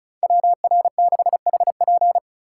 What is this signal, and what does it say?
Morse code, heard as a keyed tone, about 35 wpm, 690 Hz. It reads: WR6HP
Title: USA Call Signs - 5 Characters - 2x Character Spacing - 35wpm